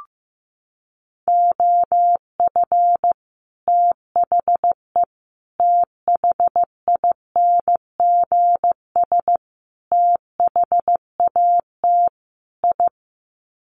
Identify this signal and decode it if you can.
Morse code, heard as a keyed tone, about 15 wpm, 700 Hz. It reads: OF THE THINGS THAT I
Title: Sets of 5 Words Encore 15wpm